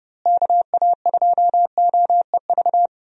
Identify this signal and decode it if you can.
Morse code, heard as a keyed tone, about 30 wpm, 700 Hz. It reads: KA2OEV
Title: USA Call Signs - 6 Characters - Repeated 2x - 30wpm